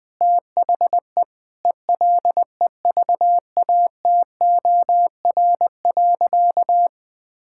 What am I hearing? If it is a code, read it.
Morse code, heard as a keyed tone, about 21 wpm, 700 Hz. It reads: THE ELEVATOR.
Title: Sentences from Top 1600 Words 20wpm